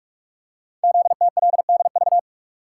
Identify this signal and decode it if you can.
Morse code, heard as a keyed tone, about 44 wpm, 700 Hz. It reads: ZTLDV
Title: Code Groups - 5 Letters - 45wpm